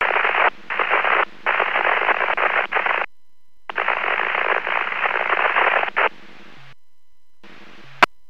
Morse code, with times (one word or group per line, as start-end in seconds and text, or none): none